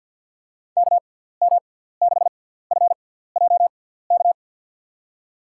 Morse code, heard as a keyed tone, about 50 wpm, spaced as for 17 wpm, 700 Hz. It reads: KM6FJK
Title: USA Call Signs - 6x Character Spacing - 50wpm